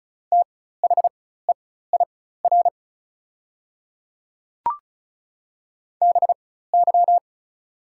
Morse code, heard as a keyed tone, about 34 wpm, spaced as for 16 wpm, 690 Hz. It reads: THEIR BY
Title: 4x Character Spacing - Top 100 Words - 35wpm